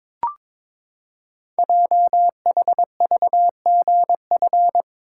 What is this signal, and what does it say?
Morse code, heard as a keyed tone, about 22 wpm, 700 Hz. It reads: JHVGF
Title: Code Groups - 5 Letters - 22wpm